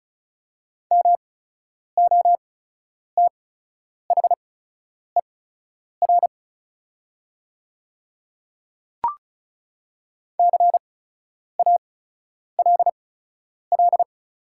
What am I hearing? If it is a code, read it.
Morse code, heard as a keyed tone, about 34 wpm, spaced as for 10 wpm, 700 Hz. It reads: MOTHER CALL
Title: 8x Character Spacing - Top 300 Words - 35wpm